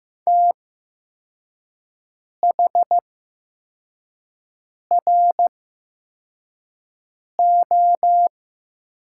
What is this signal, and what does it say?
Morse code, heard as a keyed tone, about 15 wpm, spaced as for 4 wpm, 700 Hz. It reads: THRO
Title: 8x Character Spacing - Top 500 Words - 15wpm